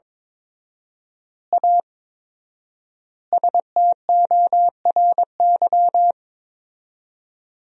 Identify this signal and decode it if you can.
Morse code, heard as a keyed tone, about 22 wpm, 700 Hz. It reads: A STORY
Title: 4x Word-Spacing - Sentences from Top 500 Words - 22wpm